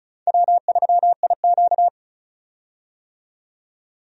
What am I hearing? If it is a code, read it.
Morse code, heard as a keyed tone, about 36 wpm, 700 Hz. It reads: W3IQ
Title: USA Call Signs - 4 Characters - 35wpm